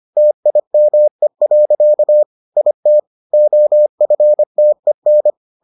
Morse code, heard as a keyed tone, about 25 wpm, 600 Hz. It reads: TIME. IT OFTEN